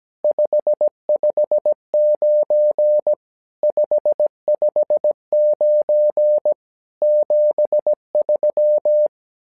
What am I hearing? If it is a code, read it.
Morse code, heard as a keyed tone, about 17 wpm, 600 Hz. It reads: 559 559 73
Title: SOTA QSO Practice 17wpm